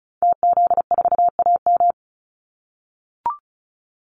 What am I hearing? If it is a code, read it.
Morse code, heard as a keyed tone, about 34 wpm, 700 Hz. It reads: TZ4AM
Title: Call Signs - 35wpm